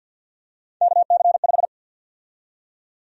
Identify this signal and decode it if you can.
Morse code, heard as a keyed tone, about 49 wpm, 700 Hz. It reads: KC5
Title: USA Call Sign Prefixes - 50wpm